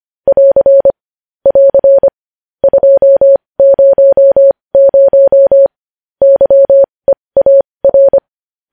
Morse code, heard as a keyed tone, about 25 wpm, spaced as for 20 wpm, 550 Hz. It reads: <AR> <AR> 200 YEAR
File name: podcast / News headlines-1x-25x20